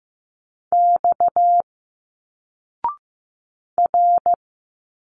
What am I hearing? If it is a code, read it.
Morse code, heard as a keyed tone, about 15 wpm, 700 Hz. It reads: X R
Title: Single Letters 15wpm